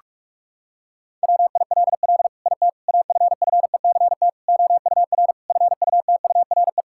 Morse code, heard as a keyed tone, about 45 wpm, 700 Hz. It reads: WILL IT AFFECT OUR FUTURE
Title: Sentences from Top 1000 Words - Review Part 6 - 45wpm